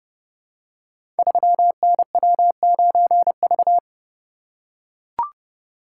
Morse code, heard as a keyed tone, about 30 wpm, 700 Hz. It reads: 3NW9V
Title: Code Groups - 5 Letter:Numbers - 30wpm